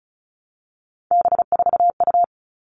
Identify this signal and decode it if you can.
Morse code, heard as a keyed tone, about 34 wpm, 700 Hz. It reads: B4U
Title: All Cut Numbers - 3-Digit 35wpm